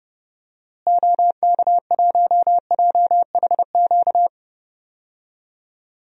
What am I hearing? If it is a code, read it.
Morse code, heard as a keyed tone, about 30 wpm, 700 Hz. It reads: OK1JHQ
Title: DXCC-100 Call Signs - 6 Characters - Repeated 2x - 30wpm